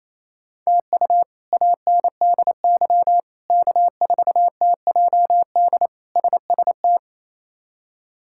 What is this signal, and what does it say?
Morse code, heard as a keyed tone, about 28 wpm, 700 Hz. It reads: TU ANDY K4TJD SST
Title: K1USN SST QSO Practice - 28wpm